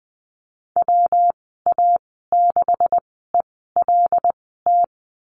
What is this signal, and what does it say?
Morse code, heard as a keyed tone, about 20 wpm, spaced as for 14 wpm, 700 Hz. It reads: WA6ELT
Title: 167 Lesson 27g - Number 7 - Call Signs 20wpm